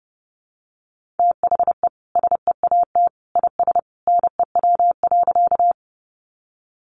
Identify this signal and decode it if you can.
Morse code, heard as a keyed tone, about 30 wpm, 700 Hz. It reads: THE SEAT IS NEW.
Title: Sentences from Top 1500 Words 30wpm